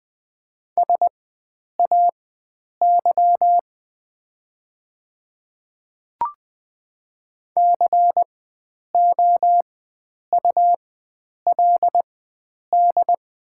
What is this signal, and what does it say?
Morse code, heard as a keyed tone, about 20 wpm, spaced as for 9 wpm, 700 Hz. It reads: SAY COULD
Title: 4x Character Spacing - Top 300 Words - 20wpm